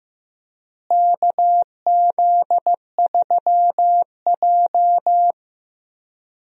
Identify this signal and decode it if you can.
Morse code, heard as a keyed tone, about 15 wpm, 700 Hz. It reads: KZ3J
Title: USA Call Signs - 4 Characters - Repeated 3x - 15wpm